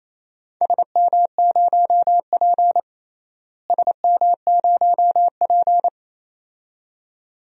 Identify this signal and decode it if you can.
Morse code, heard as a keyed tone, about 28 wpm, 700 Hz. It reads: SM0P SM0P
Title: DXCC-100 Call Signs - 4 Characters - Repeated 2x - 28wpm